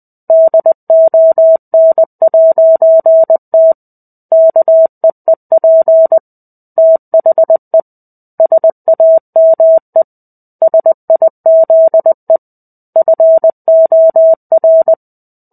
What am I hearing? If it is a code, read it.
Morse code, heard as a keyed tone, about 20 wpm, 650 Hz. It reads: DON'T KEEP THE SAME SIZE FOR
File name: 05-Chapter-05 20 WPM